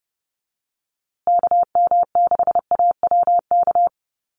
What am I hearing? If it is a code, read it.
Morse code, heard as a keyed tone, about 30 wpm, 700 Hz. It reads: KM6AWK
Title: USA Call Signs Repeated 2x 30wpm